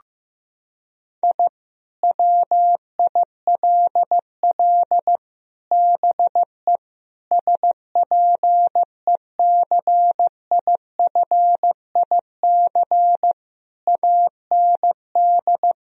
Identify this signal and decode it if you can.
Morse code, heard as a keyed tone, about 16 wpm, 700 Hz. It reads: I WILL BE SPECIFIC AND
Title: Sentences from Top 1000 Words - Review Part 3 - 15wpm